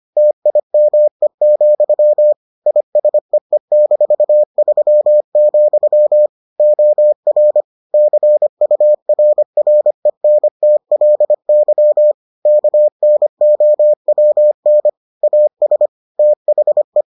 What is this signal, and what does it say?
Morse code, heard as a keyed tone, about 25 wpm, 600 Hz. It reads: TIME, ISEE-3, OR CURRENTLY KNOWN AS THE